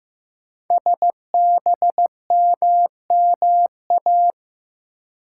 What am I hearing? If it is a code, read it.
Morse code, heard as a keyed tone, about 15 wpm, 700 Hz. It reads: SBMMA
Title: Code Groups - 5 Letters - 15wpm